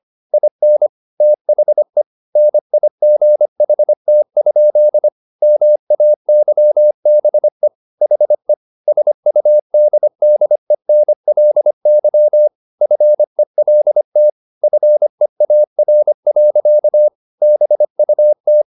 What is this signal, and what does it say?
Morse code, heard as a keyed tone, about 25 wpm, 600 Hz. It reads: IN THE NIGHT? MAYBE HE SUDDENLY FELT FEAR. BUT